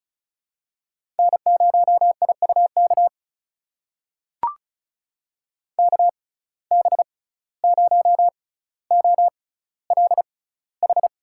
Morse code, heard as a keyed tone, about 36 wpm, spaced as for 17 wpm, 700 Hz. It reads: N0IUK KB0OLH
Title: USA Call Signs - 6x Character Spacing - 35wpm